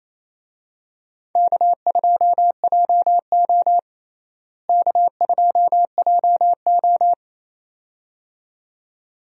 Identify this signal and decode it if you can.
Morse code, heard as a keyed tone, about 28 wpm, 700 Hz. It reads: K2JO K2JO
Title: USA Call Signs - 4 Characters - Repeated 2x - 28wpm